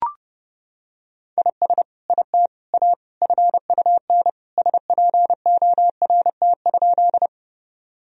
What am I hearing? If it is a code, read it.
Morse code, heard as a keyed tone, about 30 wpm, 700 Hz. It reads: IS IT A FUN SPORT?
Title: Sentences from Top 1000 Words - Review Part 4 - 30wpm